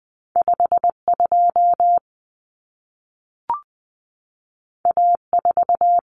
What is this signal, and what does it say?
Morse code, heard as a keyed tone, about 20 wpm, 700 Hz. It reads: 52 A4
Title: Common Cut Numbers - 2-Digit 20wpm